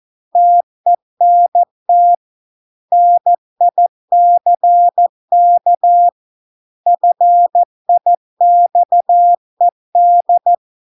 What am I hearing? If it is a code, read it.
Morse code, heard as a keyed tone, about 14 wpm, 700 Hz. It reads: TENT NICK FIXED